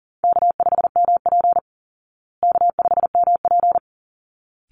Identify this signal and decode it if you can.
Morse code, heard as a keyed tone, about 40 wpm, 700 Hz. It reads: K5MP K5MP
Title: Call Signs Repeated 3x - 40wpm